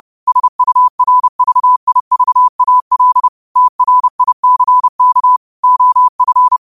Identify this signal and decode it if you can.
Morse code, heard as a keyed tone, about 30 wpm, 1000 Hz. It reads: SURVIVAL TRICK OF